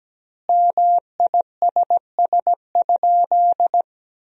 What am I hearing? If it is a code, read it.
Morse code, heard as a keyed tone, about 17 wpm, 700 Hz. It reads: MISS?
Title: 2x Word-Spacing - Sentences from Top 500 Words - 17wpm